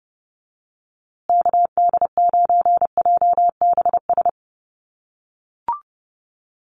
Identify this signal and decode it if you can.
Morse code, heard as a keyed tone, about 28 wpm, 700 Hz. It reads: KD9JBS
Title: USA Call Signs Repeated 3x 30wpm